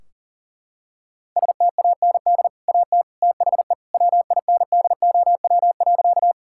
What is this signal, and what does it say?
Morse code, heard as a keyed tone, about 40 wpm, 700 Hz. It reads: STAND AT THE WINDOW.
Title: Sentences from Top 1600 Words 40wpm